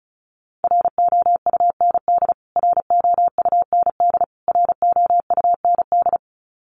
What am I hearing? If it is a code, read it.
Morse code, heard as a keyed tone, about 34 wpm, 700 Hz. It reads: ROUND ROUND ROUND
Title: Top 200 Words Repeated 35wpm